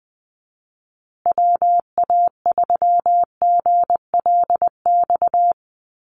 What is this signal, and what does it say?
Morse code, heard as a keyed tone, about 20 wpm, 700 Hz. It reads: WA3GLX
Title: USA Call Signs Repeated 2x 20wpm